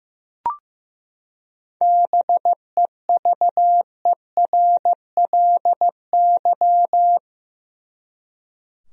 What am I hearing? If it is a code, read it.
Morse code, heard as a keyed tone, about 16 wpm, 700 Hz. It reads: BEVERLY
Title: USA Common Names - 15wpm